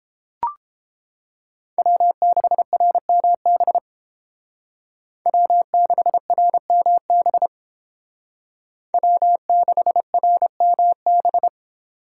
Morse code, heard as a keyed tone, about 30 wpm, 700 Hz. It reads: W6RMB W6RMB W6RMB